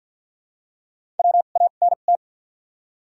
Morse code, heard as a keyed tone, about 53 wpm, spaced as for 37 wpm, 700 Hz. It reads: WANT
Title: 2x Character Spacing - Top 300 Words - 50wpm